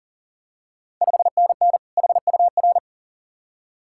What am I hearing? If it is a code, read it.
Morse code, heard as a keyed tone, about 40 wpm, 700 Hz. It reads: 5NN HUR
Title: Ontario QSO Party Exchange 40wpm